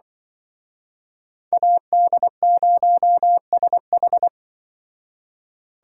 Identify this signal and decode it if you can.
Morse code, heard as a keyed tone, about 23 wpm, 700 Hz. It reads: AD0SH
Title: Speed Racing - USA Call Signs 20wpm